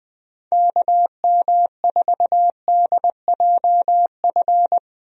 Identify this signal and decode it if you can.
Morse code, heard as a keyed tone, about 20 wpm, 700 Hz. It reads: KM4DJF
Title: USA Call Signs - 6 Characters - Repeated 2x - 20wpm